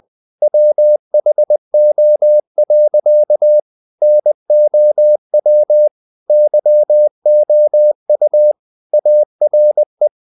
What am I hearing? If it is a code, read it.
Morse code, heard as a keyed tone, about 20 wpm, 600 Hz. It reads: WHO. NOW YOU ARE
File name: the old man and the sea part 2 20wpm